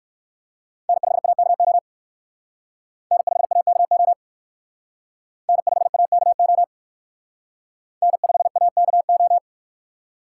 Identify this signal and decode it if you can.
Morse code, heard as a keyed tone, about 59 wpm, 700 Hz. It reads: N5AKO N5AKO N5AKO N5AKO